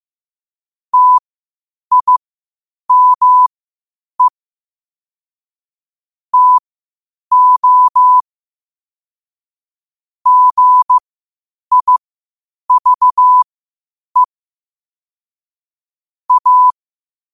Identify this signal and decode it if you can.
Morse code, heard as a keyed tone, about 15 wpm, spaced as for 8 wpm, 1000 Hz. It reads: TIME TO GIVE A
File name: qotdfiles / 8wpm